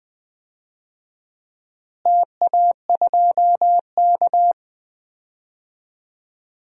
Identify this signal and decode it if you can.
Morse code, heard as a keyed tone, about 20 wpm, 700 Hz. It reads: TA2K
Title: DXCC-100 Call Signs - 4 Characters - Repeated 2x - 20wpm